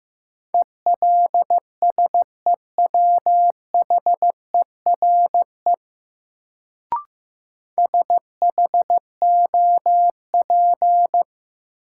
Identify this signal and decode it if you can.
Morse code, heard as a keyed tone, about 15 wpm, 700 Hz. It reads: ELSEWHERE SHOP
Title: Sentences from Top 1600 Words - New Word Drill - 15wpm